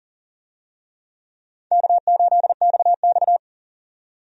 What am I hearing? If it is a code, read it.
Morse code, heard as a keyed tone, about 40 wpm, 700 Hz. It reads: K8XX
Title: USA Call Signs - 4 Characters - 40wpm